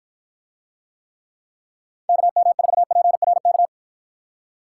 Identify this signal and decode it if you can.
Morse code, heard as a keyed tone, about 53 wpm, 700 Hz. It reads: KM4PRK